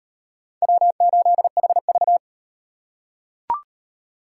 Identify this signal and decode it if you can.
Morse code, heard as a keyed tone, about 36 wpm, 700 Hz. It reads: W8HV